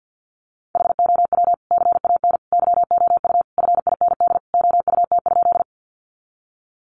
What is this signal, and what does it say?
Morse code, heard as a keyed tone, about 50 wpm, 700 Hz. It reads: HOW CAN YOU FIND OUT?
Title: Sentences from Top 200 Words 50wpm